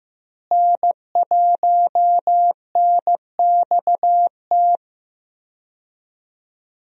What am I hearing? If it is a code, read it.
Morse code, heard as a keyed tone, about 15 wpm, 700 Hz. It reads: N1NXT
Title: USA Call Signs - 5 Characters - Repeated 2x - 15wpm